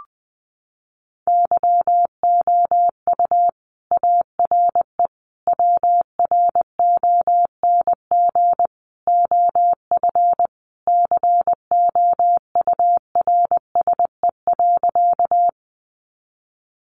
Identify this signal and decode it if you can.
Morse code, heard as a keyed tone, about 20 wpm, 700 Hz. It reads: YOU ARE WRONG OF COURSE.
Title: Sentences from Top 900 Words 20wpm